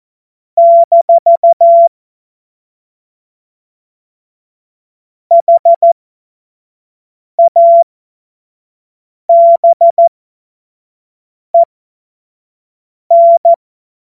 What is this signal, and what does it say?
Morse code, heard as a keyed tone, about 14 wpm, spaced as for 5 wpm, 680 Hz. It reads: - HABEN
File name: Morsekurs / Witze / Witz 25. Oktober 2022 25 F70